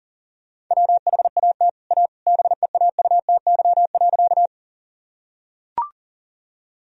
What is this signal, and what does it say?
Morse code, heard as a keyed tone, about 40 wpm, 700 Hz. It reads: WHAT A BEAUTY.
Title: Sentences from Top 1000 Words - Review Part 2 - 40wpm